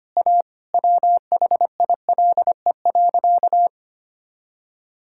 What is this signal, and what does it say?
Morse code, heard as a keyed tone, about 25 wpm, 700 Hz. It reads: A WHILE.
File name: practice / 25wpm / Sentences with 3-Word Phrases - 25wpm